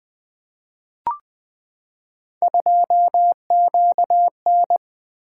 Morse code, heard as a keyed tone, about 20 wpm, 700 Hz. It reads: 2QN
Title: Three Letter-Number Combinations - 20wpm